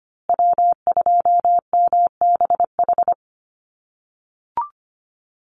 Morse code, heard as a keyed tone, about 25 wpm, 700 Hz. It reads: W2MBH
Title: USA Call Signs Repeated 2x 25wpm